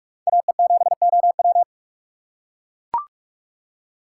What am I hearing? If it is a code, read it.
Morse code, heard as a keyed tone, about 44 wpm, 700 Hz. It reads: AE7OW
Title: USA Call Signs - 5 Characters - 6x Character Spacing - 45wpm